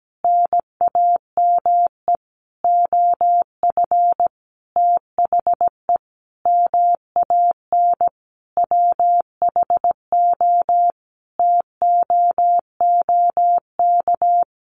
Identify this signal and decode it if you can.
Morse code, heard as a keyed tone, about 17 wpm, 700 Hz. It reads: NAME OF THE MAN WHO TOOK
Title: Sentences from Top 300 Words 17wpm